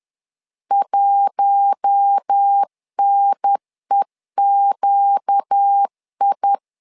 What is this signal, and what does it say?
Morse code, heard as a keyed tone, about 10 wpm, 800 Hz. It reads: 1NEQI